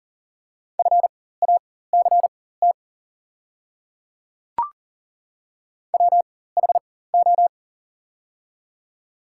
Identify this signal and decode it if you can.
Morse code, heard as a keyed tone, about 40 wpm, spaced as for 19 wpm, 700 Hz. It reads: FACT WHO
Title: 4x Character Spacing - Top 500 Words - 40wpm